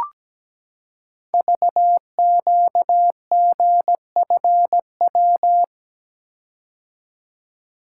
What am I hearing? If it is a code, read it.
Morse code, heard as a keyed tone, about 17 wpm, 700 Hz. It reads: VQGFW
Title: Code Groups - 5 Letters - 17wpm